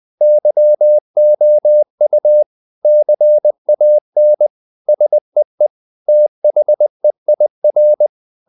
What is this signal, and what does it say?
Morse code, heard as a keyed tone, about 20 wpm, 600 Hz. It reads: YOU CAN SEE THEIR
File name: the old man and the sea part 2 20wpm